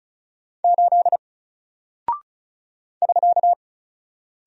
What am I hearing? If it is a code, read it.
Morse code, heard as a keyed tone, about 34 wpm, 700 Hz. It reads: 8 <SK>
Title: Single Character (Letter:Number:Punctuation:Prosign) Alt 35wpm